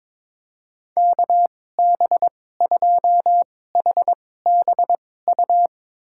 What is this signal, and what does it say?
Morse code, heard as a keyed tone, about 22 wpm, spaced as for 16 wpm, 700 Hz. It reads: KB2HBU
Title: USA Call Signs - 6 Characters - 2x Character Spacing - 22wpm